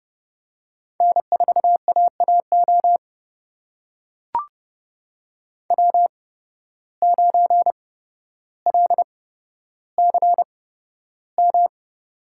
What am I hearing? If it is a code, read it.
Morse code, heard as a keyed tone, about 30 wpm, spaced as for 13 wpm, 700 Hz. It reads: N4AAO W9LCM
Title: USA Call Signs - 5 Characters - 8x Character Spacing - 30wpm